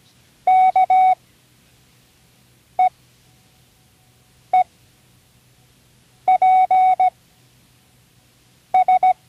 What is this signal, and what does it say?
Morse code, heard as a keyed tone, about 17 wpm, spaced as for 5 wpm, 740 Hz. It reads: KEEPS
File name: CWResources / Learning Morse Code (from Tune in the World) / 29 words using all letters